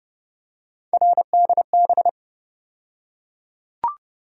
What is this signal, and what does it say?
Morse code, heard as a keyed tone, about 30 wpm, 700 Hz. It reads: RDB
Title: USA Call Sign Postfixes - 3 Letters 30wpm